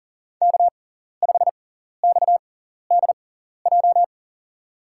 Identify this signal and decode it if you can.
Morse code, heard as a keyed tone, about 40 wpm, spaced as for 14 wpm, 700 Hz. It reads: K5XDJ
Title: USA Call Signs - 6x Character Spacing - 40wpm